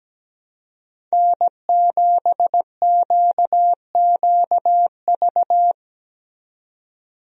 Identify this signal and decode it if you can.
Morse code, heard as a keyed tone, about 17 wpm, 700 Hz. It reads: N7QQV
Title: USA Call Signs - 5 Characters - 17wpm